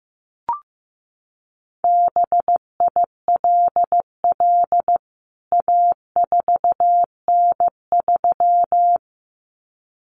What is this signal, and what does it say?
Morse code, heard as a keyed tone, about 15 wpm, 700 Hz. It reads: BILL A4N3
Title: CWT Exchange 15wpm